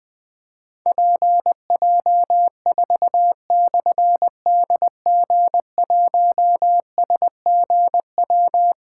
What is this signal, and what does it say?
Morse code, heard as a keyed tone, about 20 wpm, 700 Hz. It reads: PJ4/DG1SGW
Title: Difficult Call Signs 20wpm